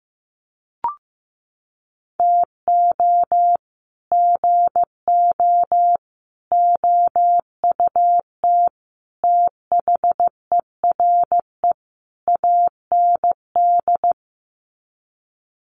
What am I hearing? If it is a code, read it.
Morse code, heard as a keyed tone, about 15 wpm, 700 Hz. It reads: TO GO OUT THERE AND
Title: Sets of 5 Words Encore2 15wpm